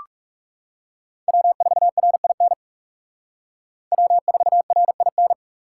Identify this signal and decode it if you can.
Morse code, heard as a keyed tone, about 36 wpm, 700 Hz. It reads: W4RIN W4RIN
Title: Speed Racing - USA 1x3 Call Signs 30wpm